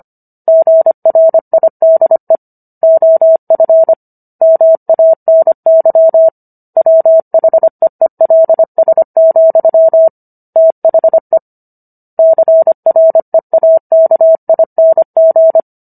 Morse code, heard as a keyed tone, about 25 wpm, 650 Hz. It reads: GRIDE OF MANY WHEELS, THE CREAKING